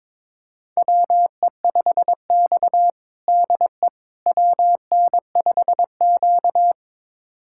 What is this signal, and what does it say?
Morse code, heard as a keyed tone, about 22 wpm, 700 Hz. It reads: WE5X DE WN5Q
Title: USA Callsign DE Callsign - 4 Characters - 22wpm